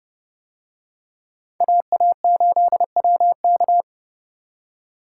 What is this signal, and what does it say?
Morse code, heard as a keyed tone, about 29 wpm, 700 Hz. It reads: AA8WK